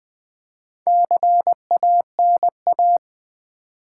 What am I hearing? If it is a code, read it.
Morse code, heard as a keyed tone, about 20 wpm, 700 Hz. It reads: CANA
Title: Build A Word - 1 Letter At A Time - 5-Letter Words - 20wpm